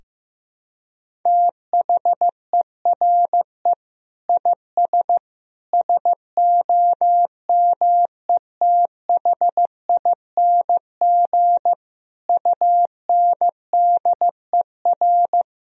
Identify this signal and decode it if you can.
Morse code, heard as a keyed tone, about 15 wpm, 700 Hz. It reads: THERE IS SOMETHING UNDER
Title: Sentences from Top 1000 Words - Review Part 1 - 15wpm